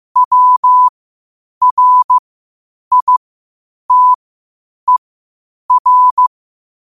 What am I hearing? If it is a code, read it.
Morse code, heard as a keyed tone, about 15 wpm, spaced as for 8 wpm, 1000 Hz. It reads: WRITER
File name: qotdfiles / 8wpm